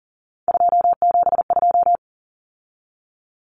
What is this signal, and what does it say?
Morse code, heard as a keyed tone, about 40 wpm, 700 Hz. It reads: 272
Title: Numbers - 3-Digit 40wpm